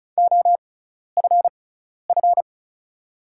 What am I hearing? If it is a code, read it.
Morse code, heard as a keyed tone, about 36 wpm, spaced as for 12 wpm, 700 Hz. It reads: OFF
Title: 6x Character Spacing - Top 200 Words - 35wpm